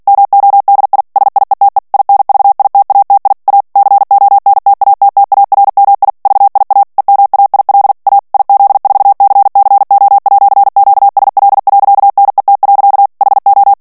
Morse code, heard as a keyed tone, about 48 wpm, 800 Hz. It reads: MODI SIETE INVITATI A CONTATTARMI VIA EMAIL A IZ4CCO@QSL.NET. HO
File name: morse mp3 / ITALIEN texte clair 1 vitesse 45 mots minute